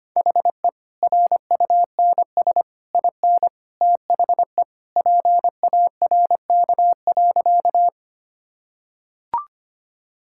Morse code, heard as a keyed tone, about 25 wpm, 690 Hz. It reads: HE RUNS IN THE PARK.
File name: Easy Sentences with Hints - 25wpm